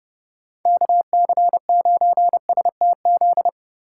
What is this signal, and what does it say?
Morse code, heard as a keyed tone, about 30 wpm, 700 Hz. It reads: KC9STZ